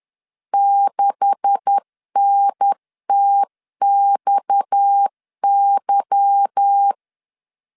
Morse code, heard as a keyed tone, about 10 wpm, 800 Hz. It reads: 6NTXY
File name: LEZIONI / 9 286 10wpm 2735